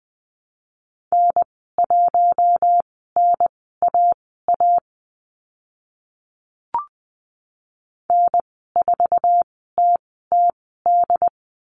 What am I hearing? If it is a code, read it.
Morse code, heard as a keyed tone, about 20 wpm, spaced as for 14 wpm, 700 Hz. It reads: N1NAA N4TTD
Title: 076 Lesson 13g - Letter L - Call Signs 20wpm